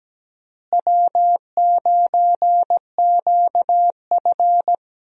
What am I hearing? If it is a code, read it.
Morse code, heard as a keyed tone, about 17 wpm, 700 Hz. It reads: W9QF
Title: USA Call Signs - 4 Characters - Repeated 3x - 17wpm